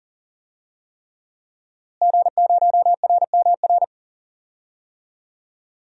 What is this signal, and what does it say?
Morse code, heard as a keyed tone, about 40 wpm, 700 Hz. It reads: G0RMR
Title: DXCC-100 Call Signs - 5 Characters - Repeated 3x - 40wpm